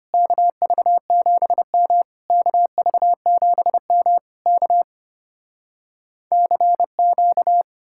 Morse code, heard as a keyed tone, about 33 wpm, 700 Hz. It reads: KV7M KV7M K CQ